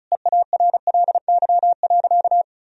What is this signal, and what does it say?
Morse code, heard as a keyed tone, about 36 wpm, 700 Hz. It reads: EARLY.
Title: Sentences from Top 1000 Words - Review Part 2 - 35wpm